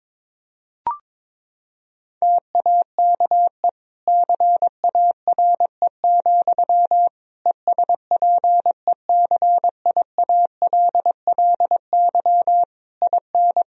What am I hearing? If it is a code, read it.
Morse code, heard as a keyed tone, about 22 wpm, 700 Hz. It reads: TAKE CARE, ESPECIALLY IN
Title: Sentences from Top 1000 Words - Review Part 4 - 22wpm